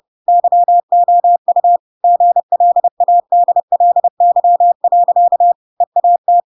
Morse code, heard as a keyed tone, about 30 wpm, 700 Hz. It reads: YOU GLADLY. EAT
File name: Pinocchio / Pinocchio ch7 30 30 ik1hge